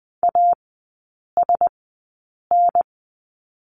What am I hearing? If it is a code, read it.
Morse code, heard as a keyed tone, about 20 wpm, spaced as for 8 wpm, 700 Hz. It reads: ASN
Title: 035 Lesson 7e - Letter s - Three Characters 20wpm